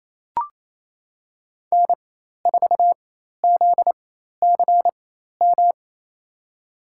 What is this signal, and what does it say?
Morse code, heard as a keyed tone, about 28 wpm, spaced as for 13 wpm, 700 Hz. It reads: N4ZCM
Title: USA Call Signs - 5 Characters - 4x Character Spacing - 28wpm